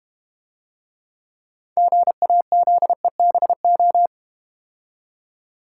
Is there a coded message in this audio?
Yes: Morse code from 1.8 to 4.1 s, heard as a keyed tone, about 32 wpm, 700 Hz. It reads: GAZEBO